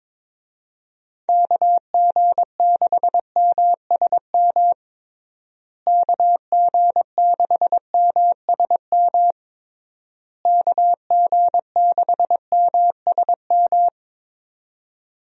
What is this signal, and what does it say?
Morse code, heard as a keyed tone, about 21 wpm, 700 Hz. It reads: KG6MSM KG6MSM KG6MSM
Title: USA Call Signs - 6 Characters - Repeated 3x - 22wpm